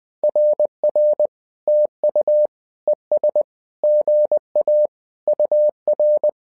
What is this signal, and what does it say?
Morse code, heard as a keyed tone, about 20 wpm, 600 Hz. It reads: RR TU ES GA UR
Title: SOTA QSO Practice 20wpm